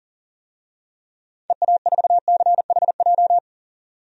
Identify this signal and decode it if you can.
Morse code, heard as a keyed tone, about 40 wpm, 700 Hz. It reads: EA4CHJ
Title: DXCC-100 Call Signs - 6 Characters - 40wpm